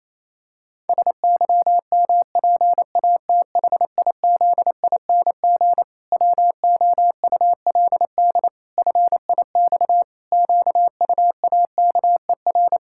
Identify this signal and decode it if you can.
Morse code, heard as a keyed tone, about 28 wpm, 700 Hz. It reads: SYMPATHIZING WOULD FIX QUAKER
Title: Pangram Challenge v2 - 28wpm